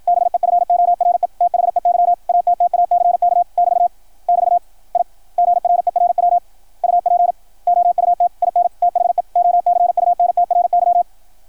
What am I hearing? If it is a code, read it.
Morse code, heard as a keyed tone, about 53 wpm, 700 Hz, with radio noise. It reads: BEFORE THEY ATTACK <BT> <BT> I GRERW UP OUT IN THE OCUNTRY
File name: archived / CRSnet